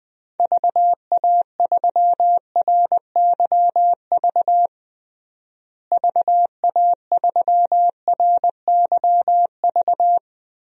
Morse code, heard as a keyed tone, about 20 wpm, 700 Hz. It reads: VA3RYV VA3RYV
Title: DXCC-100 Call Signs - 6 Characters - Repeated 2x - 20wpm